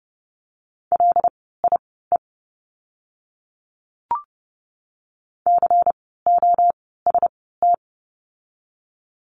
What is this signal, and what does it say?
Morse code, heard as a keyed tone, about 30 wpm, spaced as for 17 wpm, 700 Hz. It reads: LIE COST
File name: Lesson 16f - Letter c - Words 30wpm